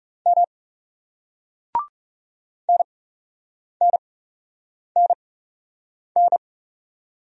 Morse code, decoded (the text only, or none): M NNNN